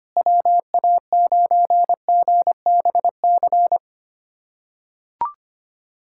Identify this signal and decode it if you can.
Morse code, heard as a keyed tone, about 25 wpm, 700 Hz. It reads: WA9GBC
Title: USA Callsign DE Callsign - 6 Characters - 25wpm